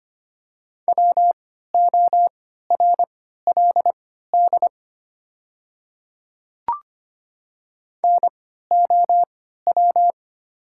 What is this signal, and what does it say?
Morse code, heard as a keyed tone, about 25 wpm, spaced as for 14 wpm, 700 Hz. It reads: WORLD NOW
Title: 3x Character Spacing - Top 500 Words - 25wpm